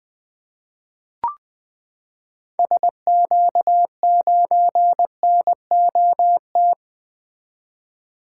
Morse code, heard as a keyed tone, about 20 wpm, 700 Hz. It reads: SQ9NOT
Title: 2023 SOTA Activator Call Signs - 20wpm